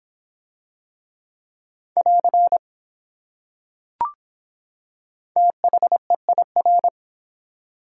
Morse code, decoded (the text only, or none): <AR> THEIR